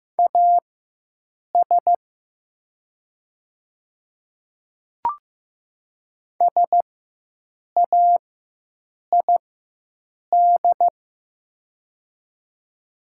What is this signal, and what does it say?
Morse code, heard as a keyed tone, about 15 wpm, spaced as for 7 wpm, 700 Hz. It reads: AS SAID
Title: 4x Character Spacing - Top 300 Words - 15wpm